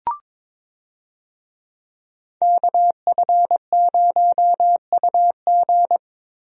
Morse code, heard as a keyed tone, about 22 wpm, 700 Hz. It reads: KF0UG
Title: USA Call Signs - 5 Characters - Repeated 3x - 22wpm